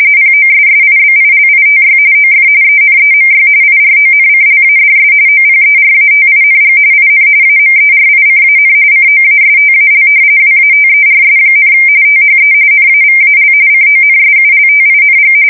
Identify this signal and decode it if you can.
RTTY, about 45.45 baud. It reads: POSSIBILITY OF ENEMY IRONCLADS OPERATING IN AMERICAN COASTAL WATERS, THE NAVAL CONSULTING
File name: fsk170 / battleships